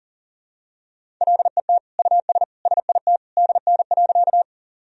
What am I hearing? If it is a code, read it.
Morse code, heard as a keyed tone, about 40 wpm, 700 Hz. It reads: LET US SIT DN.
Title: Sentences from Top 500 Words and Abbreviations 40wpm